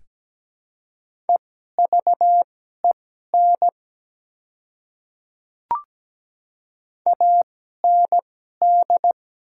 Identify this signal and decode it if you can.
Morse code, heard as a keyed tone, about 17 wpm, spaced as for 12 wpm, 700 Hz. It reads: EVEN AND
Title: 2x Character Spacing - Top 200 Words - 17wpm